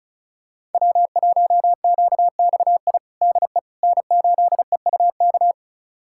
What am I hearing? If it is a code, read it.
Morse code, heard as a keyed tone, about 36 wpm, 700 Hz. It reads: W1QXI DE N8EUK
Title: USA Callsign DE Callsign - 5 Characters - 35wpm